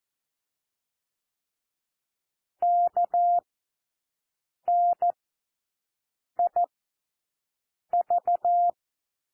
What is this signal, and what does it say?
Morse code, heard as a keyed tone, about 14 wpm, 700 Hz. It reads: K N I V
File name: cwmp3 / W1 SOUND WORDS 14 wpm 0